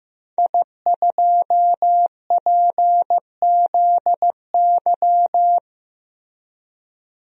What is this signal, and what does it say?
Morse code, heard as a keyed tone, about 15 wpm, 700 Hz. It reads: I2PZY
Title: DXCC-100 Call Signs - 5 Characters - 15wpm